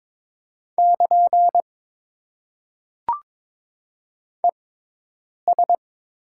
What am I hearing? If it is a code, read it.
Morse code, heard as a keyed tone, about 22 wpm, spaced as for 8 wpm, 700 Hz. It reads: <KN> ES
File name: Top 90 QSO Elements - 6x Character Spacing - 22wpm